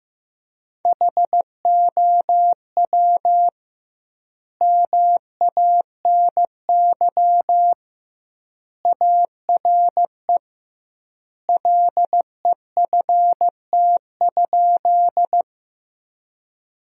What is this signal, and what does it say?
Morse code, heard as a keyed tone, about 15 wpm, 700 Hz. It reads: HOW MANY ARE LEFT?
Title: 2x Word-Spacing - Sentences from Top 300 Words - 15wpm